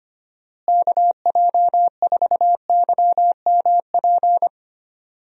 Morse code, heard as a keyed tone, about 25 wpm, 700 Hz. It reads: KJ4YMP
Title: USA Call Signs - 6 Characters - Repeated 3x - 25wpm